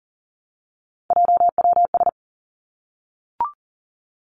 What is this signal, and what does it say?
Morse code, heard as a keyed tone, about 38 wpm, 700 Hz. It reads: JWS
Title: Three Challenging Letter-Number Combinations 40wpm